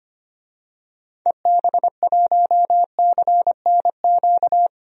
Morse code, heard as a keyed tone, about 25 wpm, 700 Hz. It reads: EB1CNQ
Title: DXCC-100 Call Signs - 6 Characters - 25wpm